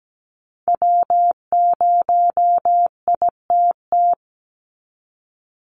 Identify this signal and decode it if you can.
Morse code, heard as a keyed tone, about 17 wpm, 700 Hz. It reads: W0ITT
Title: Call Signs Repeated 2x - 17wpm